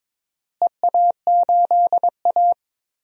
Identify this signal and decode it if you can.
Morse code, heard as a keyed tone, about 22 wpm, 700 Hz. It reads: EA8A
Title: DXCC-100 Call Signs - 4 Characters - Repeated 3x - 22wpm